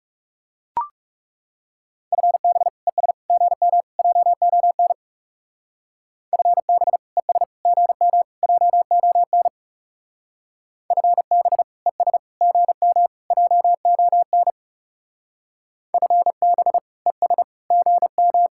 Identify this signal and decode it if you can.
Morse code, heard as a keyed tone, about 39 wpm, 700 Hz. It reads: FB ES GM JON FB ES GM JON FB ES GM JON FB ES GM